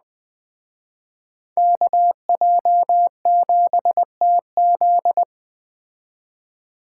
Morse code, heard as a keyed tone, about 21 wpm, 700 Hz. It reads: KJ7TZ